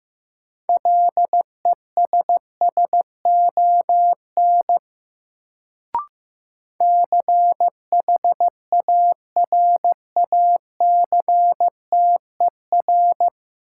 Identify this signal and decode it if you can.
Morse code, heard as a keyed tone, about 15 wpm, 700 Hz. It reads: LESSON CHARACTER
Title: Sentences from Top 1200 Words - New Word Drill - 15wpm